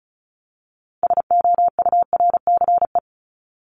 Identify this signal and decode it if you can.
Morse code, heard as a keyed tone, about 34 wpm, 700 Hz. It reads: SOURCE
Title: ES:S Suffix - Top 100 Words with Base Words 35wpm